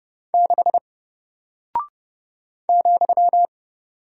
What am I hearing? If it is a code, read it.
Morse code, heard as a keyed tone, about 30 wpm, 700 Hz. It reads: 6 ,
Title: Single Character (Letter:Number:Punctuation:Prosign) 30wpm